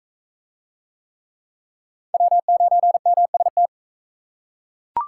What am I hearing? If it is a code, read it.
Morse code, heard as a keyed tone, about 42 wpm, 700 Hz. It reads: W9MST